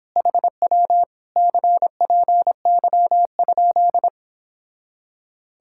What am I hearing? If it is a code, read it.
Morse code, heard as a keyed tone, about 29 wpm, 690 Hz. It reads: HW CPY?